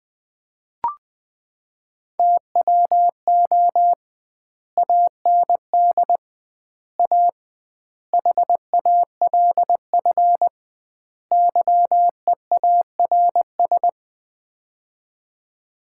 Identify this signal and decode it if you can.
Morse code, heard as a keyed tone, about 20 wpm, 700 Hz. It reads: TWO AND A HALF YEARS
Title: 2x Word-Spacing - Sets of 5 Words - 20wpm